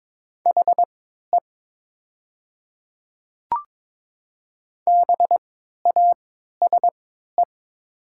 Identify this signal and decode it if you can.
Morse code, heard as a keyed tone, about 22 wpm, spaced as for 13 wpm, 700 Hz. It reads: HE BASE
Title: 3x Character Spacing - Top 300 Words - 22wpm